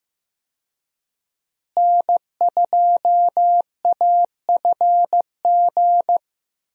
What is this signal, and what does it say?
Morse code, heard as a keyed tone, about 15 wpm, 700 Hz. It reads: N2AFG
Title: USA Call Signs - 5 Characters - Repeated 3x - 15wpm